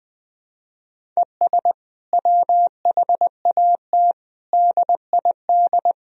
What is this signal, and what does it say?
Morse code, heard as a keyed tone, about 20 wpm, 700 Hz. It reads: ES WHAT DID
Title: Sentences from Top 300 Words and Abbreviations 20wpm